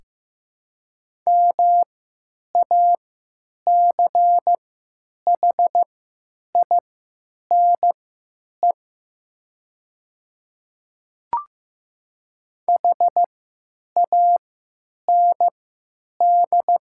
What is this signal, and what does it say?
Morse code, heard as a keyed tone, about 15 wpm, spaced as for 9 wpm, 700 Hz. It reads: MACHINE HAND
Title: 3x Character Spacing - Top 500 Words - 15wpm